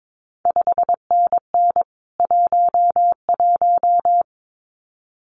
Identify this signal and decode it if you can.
Morse code, heard as a keyed tone, about 22 wpm, 700 Hz. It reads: 5NN 11
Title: CQ WW DX Contest Exchange Encore - 22wpm